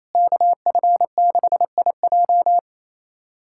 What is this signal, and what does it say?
Morse code, heard as a keyed tone, about 27 wpm, 700 Hz. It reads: KF6IJ
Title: USA Callsign DE Callsign - 5 Characters - 28wpm